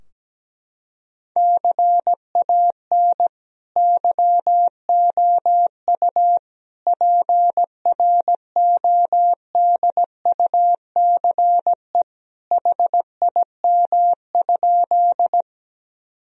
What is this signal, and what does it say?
Morse code, heard as a keyed tone, about 17 wpm, 700 Hz. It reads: CAN YOU PRODUCE HIM?
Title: Sentences from Top 1000 Words - Review Part 3 - 17wpm